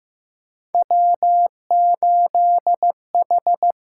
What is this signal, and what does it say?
Morse code, heard as a keyed tone, about 15 wpm, 700 Hz. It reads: W8H
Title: USA Call Signs - 4 Characters - 6x Character Spacing - 15wpm